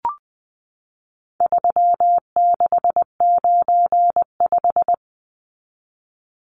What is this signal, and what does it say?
Morse code, heard as a keyed tone, about 20 wpm, 700 Hz. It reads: 3695
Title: Numbers - 4-Digit 20wpm